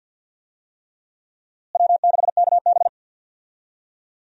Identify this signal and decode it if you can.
Morse code, heard as a keyed tone, about 50 wpm, 700 Hz. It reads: W6KB
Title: USA Call Signs - 4 Characters - Repeated 3x - 50wpm